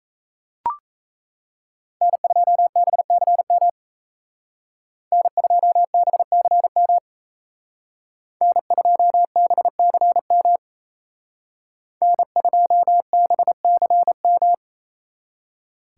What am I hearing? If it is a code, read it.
Morse code, heard as a keyed tone, about 36 wpm, 700 Hz. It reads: N2BCM N2BCM N2BCM N2BCM